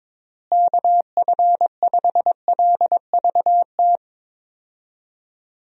Morse code, heard as a keyed tone, about 22 wpm, 700 Hz. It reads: KF5LVT
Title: USA Call Signs - 6 Characters - Repeated 3x - 22wpm